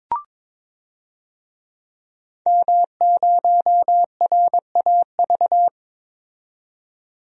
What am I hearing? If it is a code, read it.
Morse code, heard as a keyed tone, about 21 wpm, 700 Hz. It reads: M0RAV
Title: DXCC-100 Call Signs - 5 Characters - 22wpm